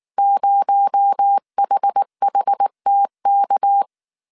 Morse code, heard as a keyed tone, about 19 wpm, 800 Hz. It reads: 0HHTK
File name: CW / DET / 1 6 1500C S A 18wpm 2255